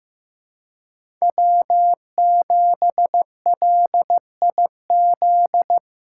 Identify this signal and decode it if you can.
Morse code, heard as a keyed tone, about 15 wpm, 700 Hz. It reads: W7LIZ
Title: USA Callsign DE Callsign - 5 Characters - 15wpm